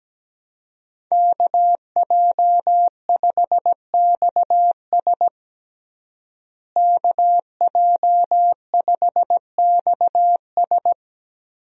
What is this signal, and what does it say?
Morse code, heard as a keyed tone, about 17 wpm, 700 Hz. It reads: KJ5XS KJ5XS
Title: USA Call Signs - 5 Characters - Repeated 2x - 17wpm